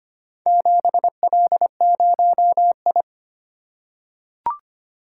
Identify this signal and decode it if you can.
Morse code, heard as a keyed tone, about 25 wpm, 700 Hz. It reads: 7L0I
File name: practice / 25wpm / Four Letter-Number Combinations - 25wpm